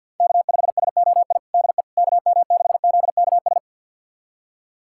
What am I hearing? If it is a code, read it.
Morse code, heard as a keyed tone, about 48 wpm, 700 Hz. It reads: K5SOI DE KM6ZKS
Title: USA Callsign DE Callsign 50wpm